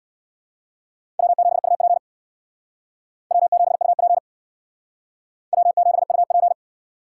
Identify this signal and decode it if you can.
Morse code, heard as a keyed tone, about 62 wpm, 700 Hz. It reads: W7UP W7UP W7UP
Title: Speed Racing - USA 1x2 Call Signs 50wpm